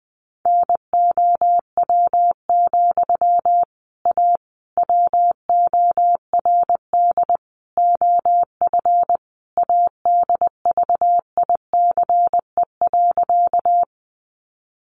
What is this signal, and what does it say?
Morse code, heard as a keyed tone, about 20 wpm, 700 Hz. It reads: NOW, A WORD OF ADVICE.
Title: Sentences from Top 1100 Words 20wpm